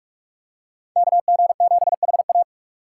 Morse code, heard as a keyed tone, about 46 wpm, 700 Hz. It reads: KG7HA
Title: USA Call Signs - 5 Characters - 2x Character Spacing - 45wpm